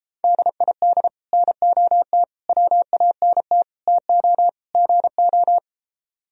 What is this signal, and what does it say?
Morse code, heard as a keyed tone, about 33 wpm, 700 Hz. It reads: DID NOT WANT TO GO